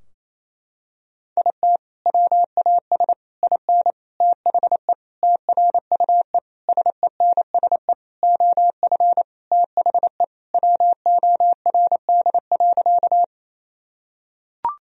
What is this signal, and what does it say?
Morse code, heard as a keyed tone, about 28 wpm, 700 Hz. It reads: IT WAS IN THE TRUE SENSE OF THE WORD.
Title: Sentences from Top 1000 Words - Review Part 4 - 28wpm